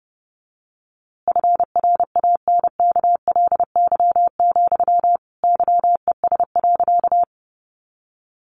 Morse code, heard as a keyed tone, about 30 wpm, 700 Hz. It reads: FRANKLY, YES.
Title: A Sound Of Thunder 30wpm